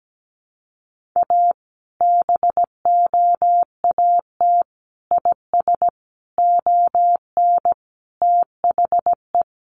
Morse code, heard as a keyed tone, about 17 wpm, 700 Hz. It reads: A BOAT IS ON THE
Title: Sentences from Top 1500 Words 17wpm